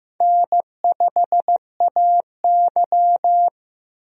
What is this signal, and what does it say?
Morse code, heard as a keyed tone, about 16 wpm, 700 Hz. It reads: N5AY
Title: USA Call Signs - 4 Characters - Repeated 2x - 15wpm